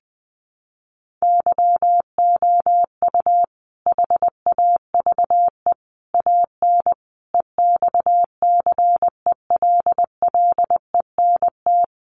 Morse code, heard as a keyed tone, about 20 wpm, 700 Hz. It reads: YOU HAVE AN EXCELLENT
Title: Sentences from Top 1300 Words 20wpm